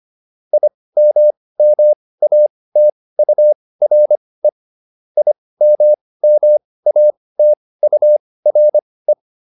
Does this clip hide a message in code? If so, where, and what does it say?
Yes: Morse code from 0.5 to 9.1 s, heard as a keyed tone, about 25 wpm, spaced as for 18 wpm, 600 Hz. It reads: IMMATURE IMMATURE